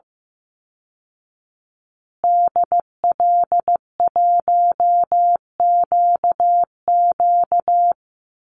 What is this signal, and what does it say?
Morse code, heard as a keyed tone, about 15 wpm, 700 Hz. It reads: DL1QQ
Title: Call Signs Repeated 3x - 15wpm